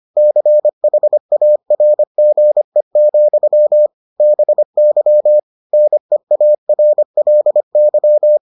Morse code, heard as a keyed tone, about 25 wpm, 600 Hz. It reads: CHARGE, BY NEARLY